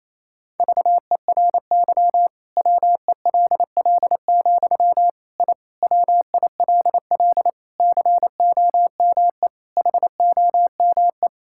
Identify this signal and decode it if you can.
Morse code, heard as a keyed tone, about 28 wpm, 700 Hz. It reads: VERY WELL, I WILL COME HOME
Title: Sentences from Top 1000 Words - Review Part 5 - 28wpm